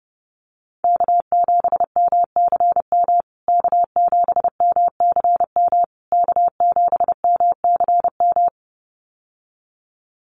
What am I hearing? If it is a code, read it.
Morse code, heard as a keyed tone, about 28 wpm, 700 Hz. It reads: K7MCM K7MCM K7MCM
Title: USA Call Signs Repeated 3x 30wpm